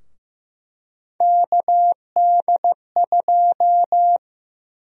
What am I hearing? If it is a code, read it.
Morse code, heard as a keyed tone, about 16 wpm, 700 Hz. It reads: KD2
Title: USA Call Sign Prefixes - Repeated 2x - 15wpm